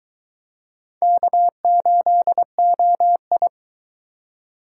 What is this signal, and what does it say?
Morse code, heard as a keyed tone, about 23 wpm, 700 Hz. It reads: K8OI